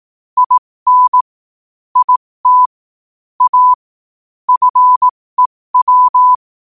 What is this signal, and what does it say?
Morse code, heard as a keyed tone, about 18 wpm, spaced as for 15 wpm, 1000 Hz. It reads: IN IT A FEW
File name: qotdfiles / 15wpm